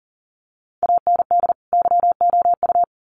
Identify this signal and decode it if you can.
Morse code, heard as a keyed tone, about 40 wpm, 700 Hz. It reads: AND YOU
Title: Sets of 2 Words 40wpm